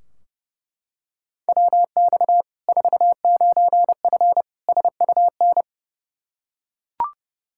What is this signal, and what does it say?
Morse code, heard as a keyed tone, about 30 wpm, 700 Hz. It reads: WX 49F SUN
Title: Practice QSO - Exchange 2 - Info - 30wpm